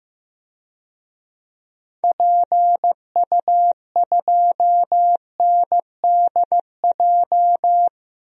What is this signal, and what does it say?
Morse code, heard as a keyed tone, about 15 wpm, 700 Hz. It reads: PU2NDJ
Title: DXCC-100 Call Signs - 6 Characters - 15wpm